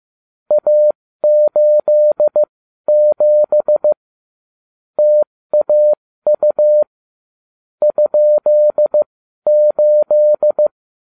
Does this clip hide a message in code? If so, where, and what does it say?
Yes: Morse code from 0.5 to 10.7 s, heard as a keyed tone, about 15 wpm, spaced as for 12 wpm, 600 Hz. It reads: A87 TAU ?8